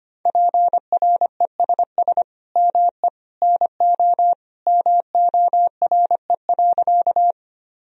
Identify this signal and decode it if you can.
Morse code, heard as a keyed tone, about 25 wpm, 700 Hz. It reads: PRESS ME NO MORE.
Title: Sentences from Top 1000 Words - Review Part 1 - 25wpm